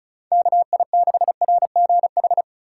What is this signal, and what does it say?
Morse code, heard as a keyed tone, about 36 wpm, 700 Hz. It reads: KI6RGH
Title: Speed Racing - USA 2x3 Call Signs 30wpm